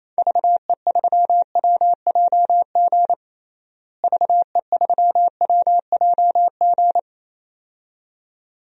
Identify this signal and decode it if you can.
Morse code, heard as a keyed tone, about 28 wpm, 700 Hz. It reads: VE3WJG VE3WJG
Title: DXCC-100 Call Signs - 6 Characters - Repeated 2x - 28wpm